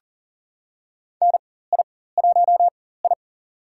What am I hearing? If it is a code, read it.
Morse code, heard as a keyed tone, about 40 wpm, spaced as for 19 wpm, 700 Hz. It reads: NI1I
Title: USA Call Signs - 4 Characters - 4x Character Spacing - 40wpm